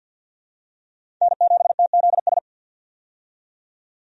Morse code, heard as a keyed tone, about 47 wpm, 700 Hz. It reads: N7TZS
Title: USA Call Signs - 5 Characters - 50wpm